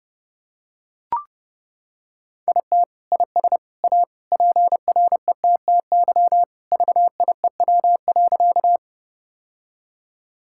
Morse code, heard as a keyed tone, about 30 wpm, 700 Hz. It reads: IT IS A PRETTY VIEW.
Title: Sentences from Top 1000 Words - Review Part 6 - 30wpm